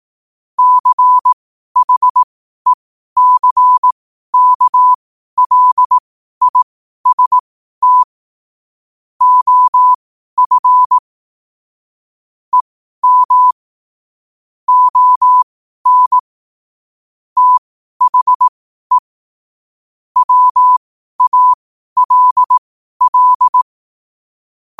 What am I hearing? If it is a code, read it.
Morse code, heard as a keyed tone, about 18 wpm, spaced as for 12 wpm, 1000 Hz. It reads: CHECKLIST OF EM ON THE WALL
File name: qotdfiles / 12wpm